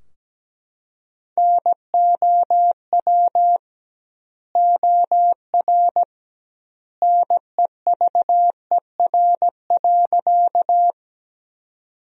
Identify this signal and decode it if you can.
Morse code, heard as a keyed tone, about 17 wpm, 700 Hz. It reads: NOW OR NEVER.
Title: 2x Word-Spacing - Sentences from Top 300 Words - 17wpm